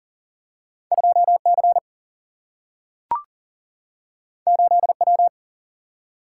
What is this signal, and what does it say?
Morse code, heard as a keyed tone, about 40 wpm, 700 Hz. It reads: 2 C 8 W
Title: Single-Letter Single-Number 40wpm